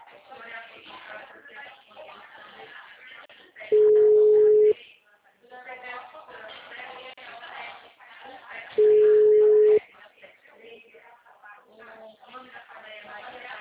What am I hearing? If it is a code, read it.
Ringback tone (Europe, 425 Hz), with speech.